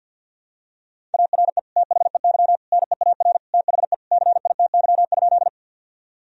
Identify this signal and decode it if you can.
Morse code, heard as a keyed tone, about 50 wpm, 700 Hz. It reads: ARE THEY NEAR THE CITY?
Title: Sentences from Top 1000 Words - Review Part 1 - 50wpm